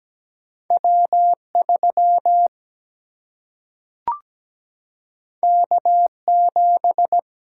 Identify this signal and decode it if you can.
Morse code, heard as a keyed tone, about 17 wpm, 700 Hz. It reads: W3 K7
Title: Single-Letter Single-Number 17wpm